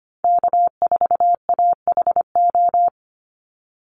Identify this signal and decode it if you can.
Morse code, heard as a keyed tone, about 25 wpm, 700 Hz. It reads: K4AHO
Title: Call Signs - 25wpm